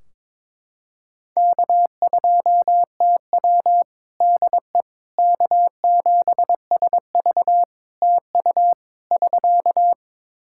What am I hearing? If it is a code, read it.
Morse code, heard as a keyed tone, about 22 wpm, 700 Hz. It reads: K2TW DE K7SV TU <SK>
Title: K1USN SST QSO Practice 22wpm